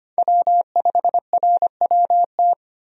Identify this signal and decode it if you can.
Morse code, heard as a keyed tone, about 25 wpm, 690 Hz. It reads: W5RWT
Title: USA Call Signs - 5 Characters - 25wpm